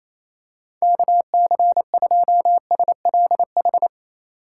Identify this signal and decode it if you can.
Morse code, heard as a keyed tone, about 28 wpm, 700 Hz. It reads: KC2SLH
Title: USA Call Signs - 6 Characters - 8x Character Spacing - 28wpm